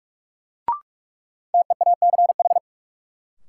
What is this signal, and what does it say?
Morse code, heard as a keyed tone, about 42 wpm, 700 Hz. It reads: TEACH
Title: Sentences from Top 500 Words - New Word Drill - 45wpm